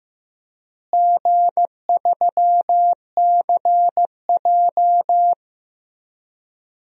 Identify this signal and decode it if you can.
Morse code, heard as a keyed tone, about 15 wpm, 700 Hz. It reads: G3CJ
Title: DXCC-100 Call Signs - 4 Characters - Repeated 3x - 15wpm